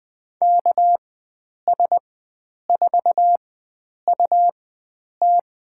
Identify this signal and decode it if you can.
Morse code, heard as a keyed tone, about 20 wpm, spaced as for 9 wpm, 700 Hz. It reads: KS4UT
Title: USA Call Signs - 5 Characters - 4x Character Spacing - 20wpm